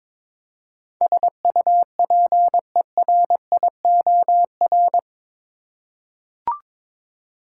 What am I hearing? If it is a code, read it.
Morse code, heard as a keyed tone, about 22 wpm, 700 Hz. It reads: SUPERIOR
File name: practice / 22wpm / Eight Letter Words - 22wpm